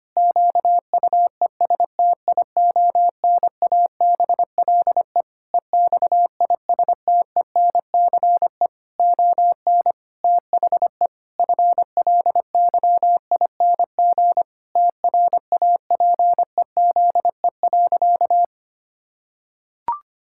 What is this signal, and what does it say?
Morse code, heard as a keyed tone, about 25 wpm, 700 Hz. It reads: QUESTIONABLE EXISTENCE ON THE FLYING TRAPEZE.
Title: Pangram Challenge v2 - 25wpm